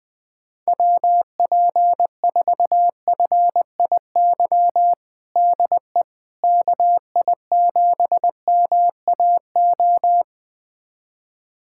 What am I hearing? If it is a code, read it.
Morse code, heard as a keyed tone, about 20 wpm, 700 Hz. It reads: WP4FIY DE KI7MAO
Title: USA Callsign DE Callsign 20wpm